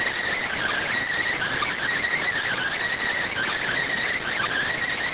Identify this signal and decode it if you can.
Emergency Alert System (SAME) tones (header bursts), heard as harsh buzzing bursts. It is heard from 0.3 to 4.7 s.